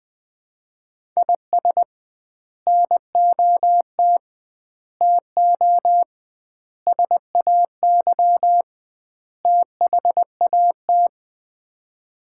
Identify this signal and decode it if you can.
Morse code, heard as a keyed tone, about 20 wpm, 700 Hz. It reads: IS NOT TO SAY THAT
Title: 2x Word-Spacing - Sets of 5 Words - 20wpm